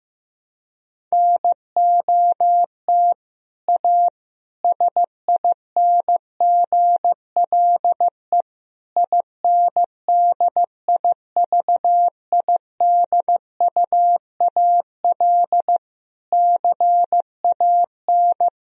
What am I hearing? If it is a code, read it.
Morse code, heard as a keyed tone, about 15 wpm, 700 Hz. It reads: NOT A SINGLE INDIVIDUAL CAN
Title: Sentences from Top 1000 Words - Review Part 4 - 15wpm